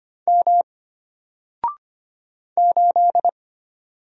Morse code, heard as a keyed tone, about 25 wpm, 700 Hz. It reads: M 8
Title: Single Character (Letter:Number:Punctuation:Prosign) Alt 25wpm